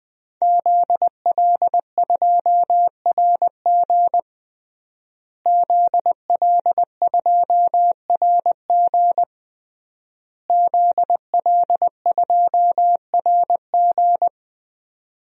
Morse code, heard as a keyed tone, about 20 wpm, 700 Hz. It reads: ZL2RG ZL2RG ZL2RG
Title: DXCC-100 Call Signs - 5 Characters - Repeated 3x - 20wpm